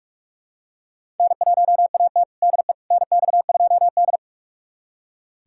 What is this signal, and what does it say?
Morse code, heard as a keyed tone, about 42 wpm, 700 Hz. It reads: N1AT DE NX2D
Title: USA Callsign DE Callsign - 4 Characters - 45wpm